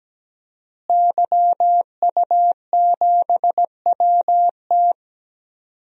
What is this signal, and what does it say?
Morse code, heard as a keyed tone, about 17 wpm, 700 Hz. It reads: YU7WT
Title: DXCC-100 Call Signs - 5 Characters - Repeated 2x - 17wpm